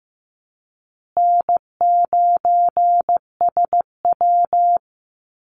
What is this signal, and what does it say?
Morse code, heard as a keyed tone, about 15 wpm, 700 Hz. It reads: N9SW
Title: Easy Call Signs Repeated 2x - 15wpm